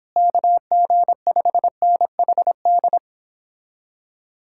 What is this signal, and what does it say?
Morse code, heard as a keyed tone, about 29 wpm, 700 Hz. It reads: KG5NHD